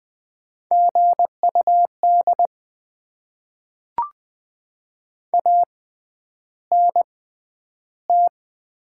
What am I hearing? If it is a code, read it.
Morse code, heard as a keyed tone, about 20 wpm, spaced as for 10 wpm, 690 Hz. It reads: GUD ANT
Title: Top 20 QSO Elements - 6x Character Spacing - 20wpm